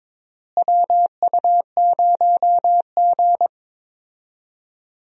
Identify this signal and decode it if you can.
Morse code, heard as a keyed tone, about 22 wpm, 700 Hz. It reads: WU0G
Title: USA Call Signs - 4 Characters - Repeated 2x - 22wpm